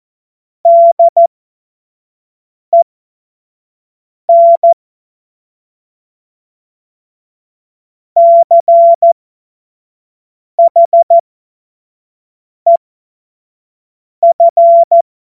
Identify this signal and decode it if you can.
Morse code, heard as a keyed tone, about 14 wpm, spaced as for 5 wpm, 680 Hz. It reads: DEN CHEF
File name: Morsekurs / Witze / Witz 17. Januar 2023 25 F70